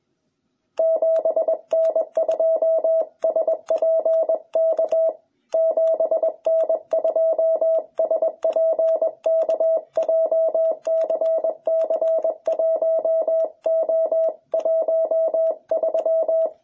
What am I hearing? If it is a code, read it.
Morse code, heard as a keyed tone, about 21 wpm, 660 Hz. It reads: 7N2SPK 7N2SPK JCC1O13